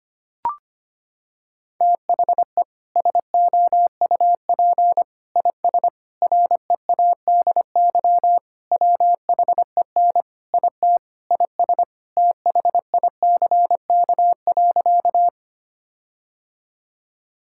Morse code, heard as a keyed tone, about 25 wpm, 700 Hz. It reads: THE SOUP IS READY WHEN IT IS THICK.
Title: Sentences from Top 1600 Words 25wpm